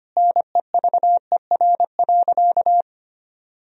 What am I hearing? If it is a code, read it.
Morse code, heard as a keyed tone, about 25 wpm, 700 Hz. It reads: NEVER.
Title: Sentences from Top 400 Words and Abbreviations 25wpm